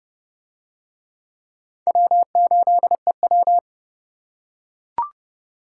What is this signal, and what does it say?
Morse code, heard as a keyed tone, about 29 wpm, 700 Hz. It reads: W8EW